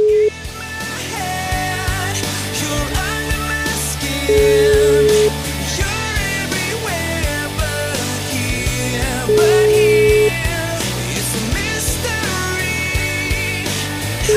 Ringback tone (Europe, 425 Hz), with noise.